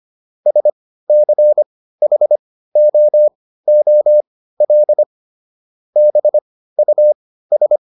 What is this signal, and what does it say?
Morse code, heard as a keyed tone, about 25 wpm, spaced as for 15 wpm, 600 Hz. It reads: SCHOOL BUS